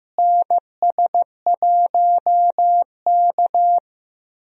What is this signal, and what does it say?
Morse code, heard as a keyed tone, about 16 wpm, 700 Hz. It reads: NS1K
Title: USA Call Signs - 4 Characters - Repeated 3x - 15wpm